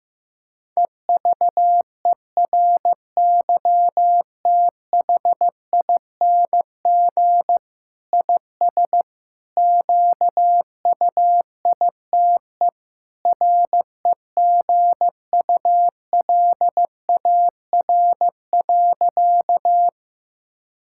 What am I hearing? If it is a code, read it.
Morse code, heard as a keyed tone, about 15 wpm, 700 Hz. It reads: EVERYTHING IS QUITE REGULAR.
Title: Sentences from Top 1000 Words - Review Part 6 - 15wpm